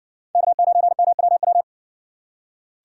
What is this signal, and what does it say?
Morse code, heard as a keyed tone, about 60 wpm, 700 Hz. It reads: K9MWW